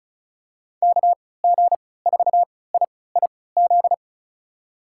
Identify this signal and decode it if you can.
Morse code, heard as a keyed tone, about 36 wpm, spaced as for 20 wpm, 700 Hz. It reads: KG4IIZ
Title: USA Call Signs - 6 Characters - 3x Character Spacing - 35wpm